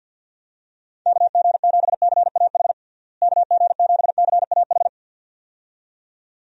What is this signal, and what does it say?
Morse code, heard as a keyed tone, about 52 wpm, 700 Hz. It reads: KG7CAH KG7CAH
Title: USA Call Signs - 6 Characters - Repeated 2x - 50wpm